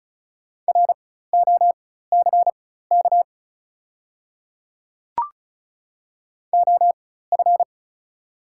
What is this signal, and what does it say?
Morse code, heard as a keyed tone, about 36 wpm, spaced as for 16 wpm, 690 Hz. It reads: ROCK OF
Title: 4x Character Spacing - Top 500 Words - 35wpm